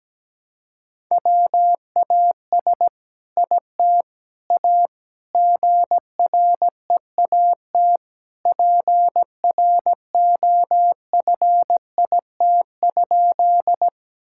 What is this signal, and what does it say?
Morse code, heard as a keyed tone, about 17 wpm, 700 Hz. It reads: WAS IT A GREAT PROFIT?
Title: Sentences from Top 1000 Words - Review Part 5 - 17wpm